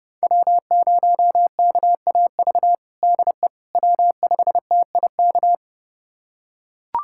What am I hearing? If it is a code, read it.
Morse code, heard as a keyed tone, about 30 wpm, 700 Hz. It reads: W0KAV DE W5TIK
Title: USA Callsign DE Callsign - 5 Characters - 30wpm